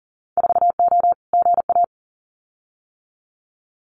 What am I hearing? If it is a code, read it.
Morse code, heard as a keyed tone, about 42 wpm, 700 Hz. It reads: 4O GA
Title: Winter Field Day Exchanges 40wpm